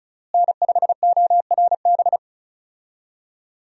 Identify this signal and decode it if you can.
Morse code, heard as a keyed tone, about 36 wpm, 700 Hz. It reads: N5ORB
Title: USA Call Signs - 5 Characters - Repeated 2x - 35wpm